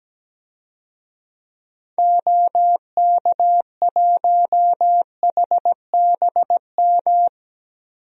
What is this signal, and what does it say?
Morse code, heard as a keyed tone, about 17 wpm, 700 Hz. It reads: OK1HBM
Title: DXCC-100 Call Signs - 6 Characters - Repeated 2x - 17wpm